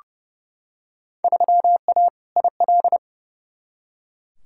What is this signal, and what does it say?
Morse code, heard as a keyed tone, about 32 wpm, 700 Hz. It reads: 3A IL
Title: Field Day Exchanges - 30wpm